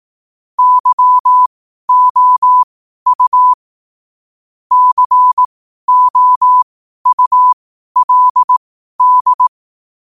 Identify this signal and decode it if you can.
Morse code, heard as a keyed tone, about 18 wpm, spaced as for 12 wpm, 1000 Hz. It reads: YOU COULD
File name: qotdfiles / 12wpm